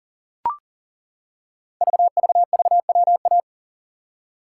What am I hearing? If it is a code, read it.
Morse code, heard as a keyed tone, about 42 wpm, 700 Hz. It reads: VVVWA
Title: Code Groups - 5 Letters - 40wpm